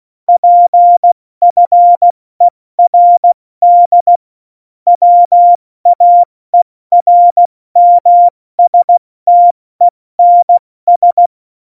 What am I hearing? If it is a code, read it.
Morse code, heard as a keyed tone, about 16 wpm, 700 Hz. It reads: PFERD WAERMSTENS
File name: Morsekurs / Witze / Witz 26 Mai, 2020 70 F80